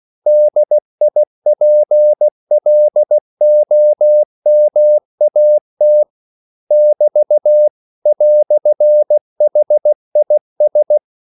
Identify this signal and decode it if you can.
Morse code, heard as a keyed tone, about 16 wpm, 600 Hz. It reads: DIPLOMAT <BT> "HIS
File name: CW / Practice / EnglishQuotations0116